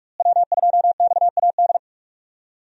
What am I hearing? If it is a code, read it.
Morse code, heard as a keyed tone, about 47 wpm, 700 Hz. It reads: W2XAD
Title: USA Call Signs - 5 Characters - Repeated 3x - 45wpm